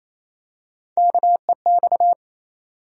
Morse code, heard as a keyed tone, about 28 wpm, 700 Hz. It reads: KEX
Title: USA Call Sign Postfixes - 3 Letters 28wpm